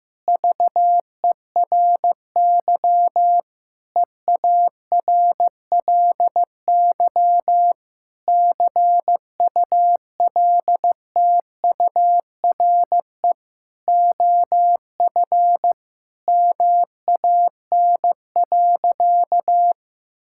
Morse code, heard as a keyed tone, about 15 wpm, 700 Hz. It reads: VERY EARLY CULTURE OF MAN.
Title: Sentences from Top 1000 Words - Review Part 6 - 15wpm